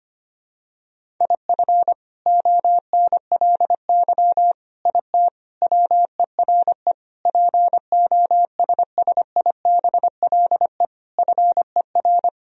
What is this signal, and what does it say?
Morse code, heard as a keyed tone, about 25 wpm, 700 Hz. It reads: IF ONLY IT WERE POSSIBLE FER
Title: Sentences from Top 500 Words and Abbreviations 25wpm